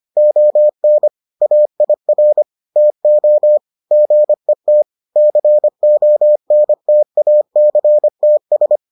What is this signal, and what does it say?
Morse code, heard as a keyed tone, about 25 wpm, 600 Hz. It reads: ON AIR TO GET CONTACTS